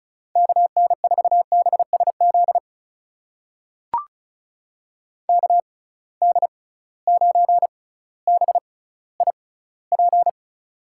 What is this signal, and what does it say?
Morse code, heard as a keyed tone, about 36 wpm, spaced as for 18 wpm, 700 Hz. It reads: KN4BSZ KD9BIP
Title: USA Call Signs - 6 Characters - 6x Character Spacing - 35wpm